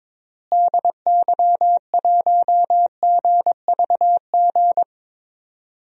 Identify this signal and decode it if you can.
Morse code, heard as a keyed tone, about 22 wpm, 700 Hz. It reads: DY1GVG
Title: DXCC-100 Call Signs - 6 Characters - Repeated 2x - 22wpm